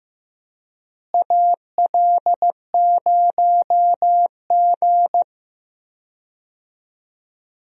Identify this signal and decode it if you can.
Morse code, heard as a keyed tone, about 16 wpm, 700 Hz. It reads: AL0G
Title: USA Call Signs - 4 Characters - Repeated 2x - 15wpm